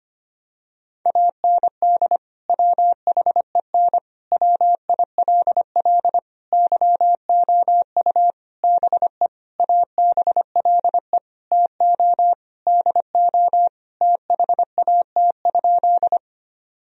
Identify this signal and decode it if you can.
Morse code, heard as a keyed tone, about 25 wpm, 700 Hz. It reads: AND WHEN WILL YOU BE ABLE TO DO THAT?
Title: Sentences from Top 1000 Words - Review Part 2 - 25wpm